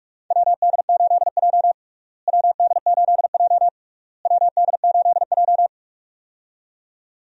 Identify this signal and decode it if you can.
Morse code, heard as a keyed tone, about 41 wpm, 700 Hz. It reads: WD8J WD8J WD8J
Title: USA Call Signs - 4 Characters - Repeated 3x - 45wpm